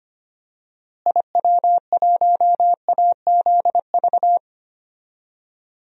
Morse code, heard as a keyed tone, about 25 wpm, 700 Hz. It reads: IW1AZV
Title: DXCC-100 Call Signs - 6 Characters - Repeated 3x - 25wpm